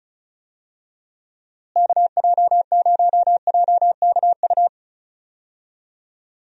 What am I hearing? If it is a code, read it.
Morse code, heard as a keyed tone, about 34 wpm, 700 Hz. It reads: KJ0JKU
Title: USA Call Signs - 6 Characters - Repeated 2x - 35wpm